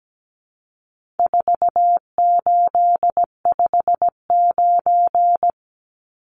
Morse code, heard as a keyed tone, about 17 wpm, 700 Hz. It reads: 4859
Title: Numbers - 4-Digit 17wpm